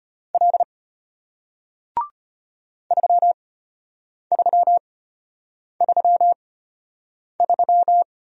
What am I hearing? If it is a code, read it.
Morse code, heard as a keyed tone, about 34 wpm, spaced as for 8 wpm, 700 Hz. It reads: L 3333